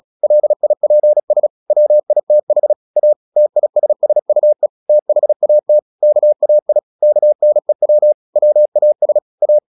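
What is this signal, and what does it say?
Morse code, heard as a keyed tone, about 36 wpm, 600 Hz. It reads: LIPS WITH A TISSUE THAT KAI KNEW WAS A
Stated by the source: Morse practice at 36 wpm, text of Ken Follett's novel Never, from CW med Gnister.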